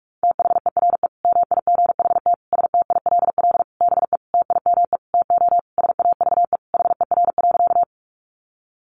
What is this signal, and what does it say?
Morse code, heard as a keyed tone, about 44 wpm, 700 Hz. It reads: THERE MIGHT STILL BE TIME TO SAVE HER.
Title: Sentences from Top 1100 Words 45wpm